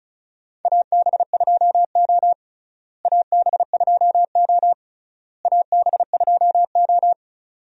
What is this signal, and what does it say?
Morse code, heard as a keyed tone, about 34 wpm, 700 Hz. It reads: AB2O AB2O AB2O
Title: USA Call Signs - 4 Characters - Repeated 3x - 35wpm